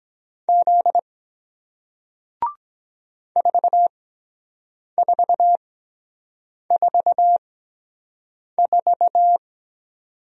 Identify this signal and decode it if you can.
Morse code, heard as a keyed tone, about 23 wpm, spaced as for 7 wpm, 700 Hz. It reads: Z 4444